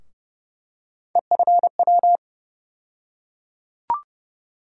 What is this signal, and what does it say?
Morse code, heard as a keyed tone, about 31 wpm, 700 Hz. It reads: EFW
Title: USA Call Sign Postfixes - 3 Letters 30wpm